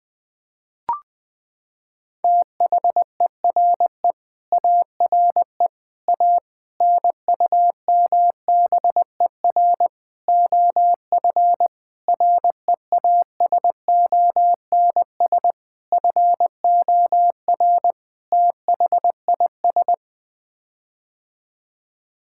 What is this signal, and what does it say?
Morse code, heard as a keyed tone, about 20 wpm, 700 Hz. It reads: THERE ARE A NUMBER OF REASONS FOR THIS
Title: Sets of 8 Words - 20wpm